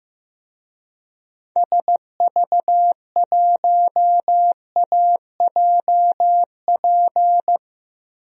Morse code, heard as a keyed tone, about 15 wpm, 700 Hz. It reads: SV1AJP
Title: DXCC-100 Call Signs - 6 Characters - Repeated 3x - 15wpm